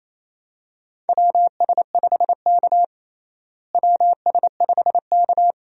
Morse code, heard as a keyed tone, about 27 wpm, 700 Hz. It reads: WS5K WS5K
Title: USA Call Signs - 4 Characters - Repeated 2x - 28wpm